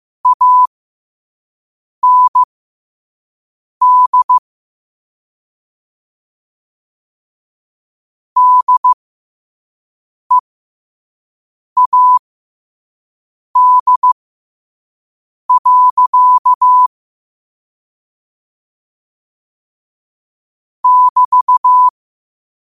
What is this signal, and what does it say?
Morse code, heard as a keyed tone, about 15 wpm, spaced as for 5 wpm, 1000 Hz. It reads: AND DEAD. <BT>
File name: qotdfiles / 5wpm